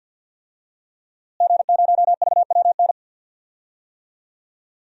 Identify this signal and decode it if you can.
Morse code, heard as a keyed tone, about 52 wpm, 700 Hz. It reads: G0VWN
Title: DXCC-100 Call Signs - 5 Characters - Repeated 2x - 50wpm